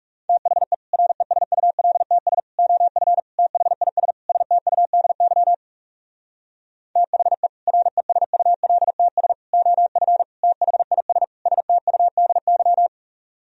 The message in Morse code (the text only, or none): THE RESULTS OF THIS STUDY THE RESULTS OF THIS STUDY